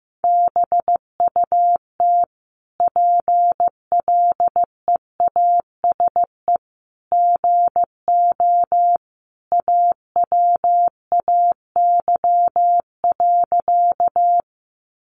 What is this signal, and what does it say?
Morse code, heard as a keyed tone, about 15 wpm, 700 Hz. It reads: BUT PLEASE GO AWAY.
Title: Sentences from Top 1500 Words 15wpm